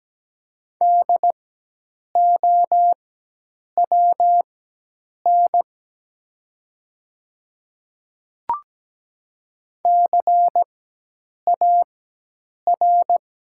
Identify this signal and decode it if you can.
Morse code, heard as a keyed tone, about 17 wpm, spaced as for 8 wpm, 700 Hz. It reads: DOWN CAR
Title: 4x Character Spacing - Top 300 Words - 17wpm